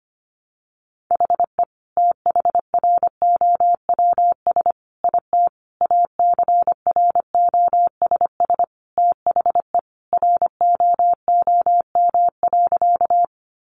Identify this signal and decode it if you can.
Morse code, heard as a keyed tone, about 25 wpm, 700 Hz. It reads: HE THROWS IT ACROSS THE ROOM.
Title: Additional Easy Phrases 25wpm